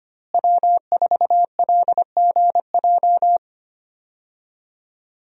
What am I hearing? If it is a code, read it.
Morse code, heard as a keyed tone, about 25 wpm, 700 Hz. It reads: W4LGJ
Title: USA Call Signs - 5 Characters - Repeated 2x - 25wpm